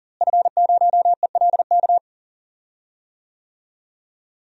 Morse code, heard as a keyed tone, about 40 wpm, 700 Hz. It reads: F0ELK
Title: DXCC-100 Call Signs - 5 Characters - Repeated 2x - 40wpm